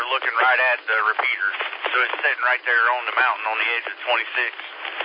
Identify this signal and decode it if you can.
Emergency Alert System (SAME) tones, heard as harsh buzzing bursts, with speech. It is heard from 1.2 to 4.0 s.